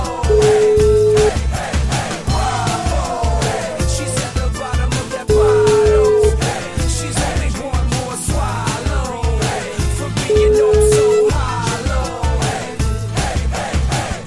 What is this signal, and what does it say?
Ringback tone (Europe, 425 Hz), with noise.